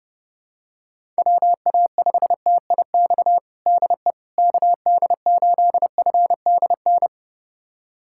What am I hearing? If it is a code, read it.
Morse code, heard as a keyed tone, about 30 wpm, 700 Hz. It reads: WA5TIX DE KD8FDN
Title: USA Callsign DE Callsign - 6 Characters - 30wpm